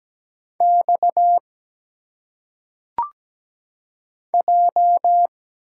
Morse code, heard as a keyed tone, about 17 wpm, 700 Hz. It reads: X J
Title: Single Letters - Phonetic Pronunciation 17wpm